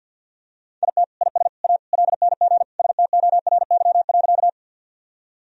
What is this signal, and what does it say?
Morse code, heard as a keyed tone, about 46 wpm, 700 Hz. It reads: IT IS A LNG STORY.
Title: Sentences from Top 300 Words and Abbreviations 50wpm